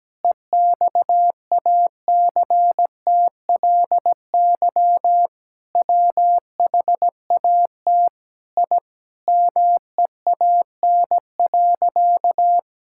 Morse code, heard as a keyed tone, about 17 wpm, 700 Hz. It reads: EXACTLY WHAT I MEAN.
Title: Sentences from Top 1000 Words - Review Part 6 - 17wpm